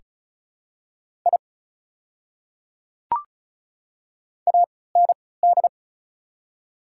Morse code, heard as a keyed tone, about 36 wpm, spaced as for 20 wpm, 700 Hz. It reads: I AND
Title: 3x Character Spacing - Top 300 Words - 35wpm